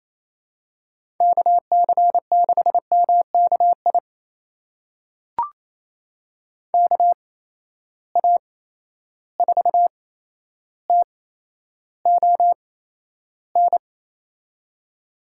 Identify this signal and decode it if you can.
Morse code, heard as a keyed tone, about 27 wpm, spaced as for 12 wpm, 700 Hz. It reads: KC6MKI KA4TON
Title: USA Call Signs - 6 Characters - 8x Character Spacing - 28wpm